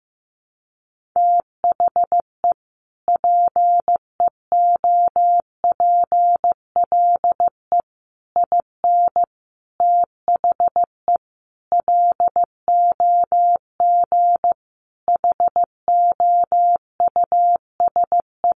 Morse code, heard as a keyed tone, about 14 wpm, 700 Hz. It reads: THE PEOPLE IN THE LOG HOUSE
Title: Sentences from Top 1400 Words 15wpm